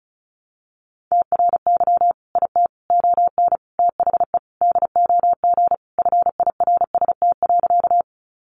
Morse code, heard as a keyed tone, about 34 wpm, 700 Hz. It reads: TRY IT ON THE DOG FIRST.
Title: Sentences from Top 400 Words 35wpm